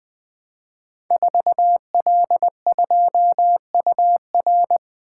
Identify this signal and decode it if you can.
Morse code, heard as a keyed tone, about 20 wpm, 700 Hz. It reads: 4L2UR
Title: Code Groups - 5 Letter:Numbers - 20wpm